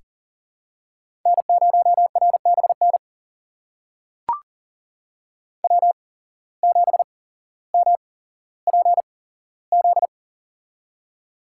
Morse code, heard as a keyed tone, about 39 wpm, spaced as for 17 wpm, 700 Hz. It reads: N0RBN W7MPZ
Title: USA Call Signs - 5 Characters - 8x Character Spacing - 40wpm